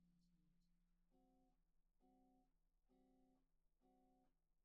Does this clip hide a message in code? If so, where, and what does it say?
No.